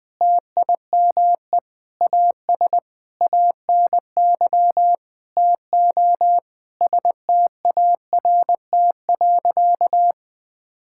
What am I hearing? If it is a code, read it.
Morse code, heard as a keyed tone, about 20 wpm, 700 Hz. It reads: TIME AS ANY TO START.
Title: Sentences with 8-Word Phrases - Spoken Beginning - 20wpm